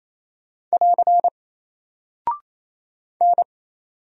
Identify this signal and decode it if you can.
Morse code, heard as a keyed tone, about 28 wpm, 700 Hz. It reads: <AR> N
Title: Single Character (Letter:Number:Punctuation:Prosign) Alt 28wpm